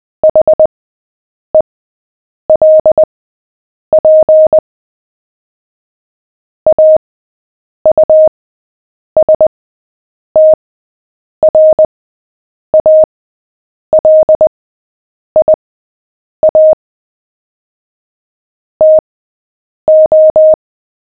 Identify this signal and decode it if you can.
Morse code, heard as a keyed tone, about 20 wpm, spaced as for 8 wpm, 620 Hz. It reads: HELP AUSTRALIA TO
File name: cw / fists / Short-Story-7-8wpm